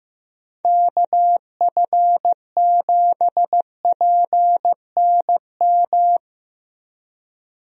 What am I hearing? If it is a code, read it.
Morse code, heard as a keyed tone, about 15 wpm, 700 Hz. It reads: KF7PNM
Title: USA Call Signs - 6 Characters - Repeated 2x - 15wpm